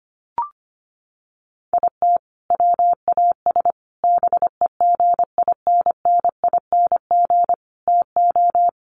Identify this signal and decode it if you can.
Morse code, heard as a keyed tone, about 25 wpm, 700 Hz. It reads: IT WAS BEGINNING TO
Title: Sentences from Top 800 Words 25wpm